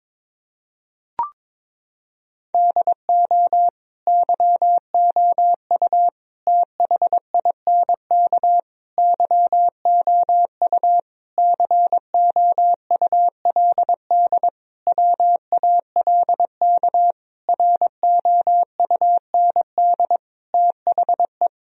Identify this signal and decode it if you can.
Morse code, heard as a keyed tone, about 22 wpm, 700 Hz. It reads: DO YOU THINK YOU COULD WALK ROUND THE
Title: Sentences from Top 1000 Words - Review Part 1 - 22wpm